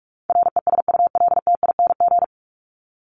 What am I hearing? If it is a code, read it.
Morse code, heard as a keyed tone, about 44 wpm, 700 Hz. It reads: RESULTING
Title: ING Suffix - Top 100 Words with Base Words 45wpm